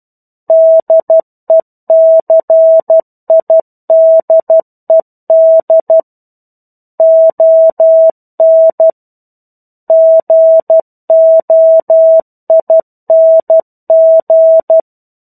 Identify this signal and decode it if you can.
Morse code, heard as a keyed tone, about 12 wpm, 650 Hz. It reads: DECIDED ON GOING
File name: 01-Chapter-01 12 WPM